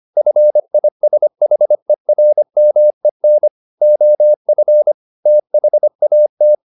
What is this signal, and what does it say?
Morse code, heard as a keyed tone, about 25 wpm, 600 Hz. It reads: FISHERMEN OF THAT
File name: the old man and the sea part 1 25wpm